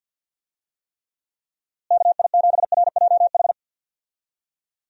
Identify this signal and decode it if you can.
Morse code, heard as a keyed tone, about 52 wpm, 700 Hz. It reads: KI7RJH
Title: USA Call Signs - 6 Characters - Repeated 3x - 50wpm